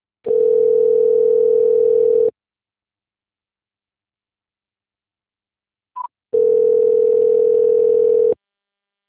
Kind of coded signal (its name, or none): telephone call progress tone